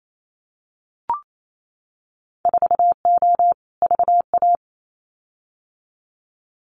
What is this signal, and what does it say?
Morse code, heard as a keyed tone, about 28 wpm, 700 Hz. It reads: 4O VA
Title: Winter Field Day Exchanges Encore 28wpm